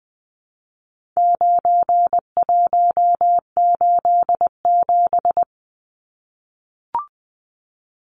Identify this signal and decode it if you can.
Morse code, heard as a keyed tone, about 20 wpm, 700 Hz. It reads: 9187
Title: Numbers - 4-Digit 20wpm